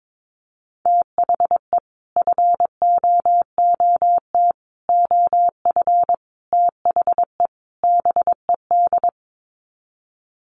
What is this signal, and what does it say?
Morse code, heard as a keyed tone, about 22 wpm, 700 Hz. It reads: THE FOOT OF THE BED
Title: Sets of 5 Words Encore2 22wpm